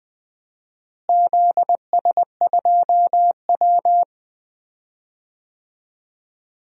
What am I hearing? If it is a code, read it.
Morse code, heard as a keyed tone, about 20 wpm, 700 Hz. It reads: ZS2W
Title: DXCC-100 Call Signs - 4 Characters - Repeated 3x - 20wpm